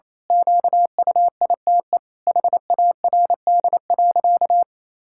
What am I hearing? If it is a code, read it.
Morse code, heard as a keyed tone, about 27 wpm, 700 Hz. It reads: QUITE HARD.
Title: Pangram Challenge v2 - 28wpm